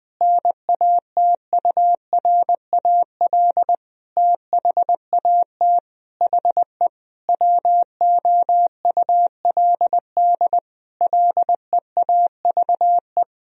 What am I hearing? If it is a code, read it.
Morse code, heard as a keyed tone, about 19 wpm, 700 Hz. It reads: NATURAL THAT HE WOULD LEAVE
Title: Sentences from Top 1000 Words - Review Part 4 - 20wpm